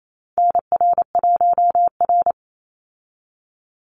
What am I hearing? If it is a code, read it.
Morse code, heard as a keyed tone, about 28 wpm, 690 Hz. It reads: NR1R
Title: Call Signs Repeated 3x - 28wpm